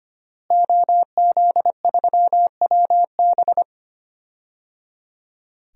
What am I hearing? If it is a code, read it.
Morse code, heard as a keyed tone, about 25 wpm, 700 Hz. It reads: OZ3WB
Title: DXCC-100 Call Signs - 5 Characters - Repeated 3x - 25wpm